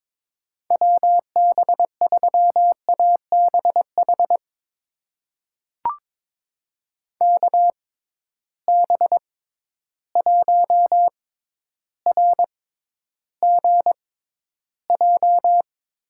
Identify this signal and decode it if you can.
Morse code, heard as a keyed tone, about 22 wpm, spaced as for 11 wpm, 700 Hz. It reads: WB3ABH KB1RGJ
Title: USA Call Signs - 6 Characters - 6x Character Spacing - 22wpm